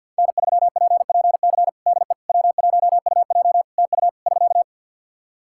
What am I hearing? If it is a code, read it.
Morse code, heard as a keyed tone, about 50 wpm, 700 Hz. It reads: N3PPC DE W1UJ TU <SK>
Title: Practice QSO - Exchange 4 - Ending - 50wpm